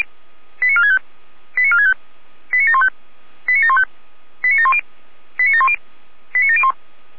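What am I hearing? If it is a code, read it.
Five-tone selective call (ZVEI), heard as quick runs of five tones, tones of about 70 ms. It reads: 89465 89465 89715 89715 89710 89710 89701
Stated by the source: Received on SDR, ZVEI five-tone selective call 89531.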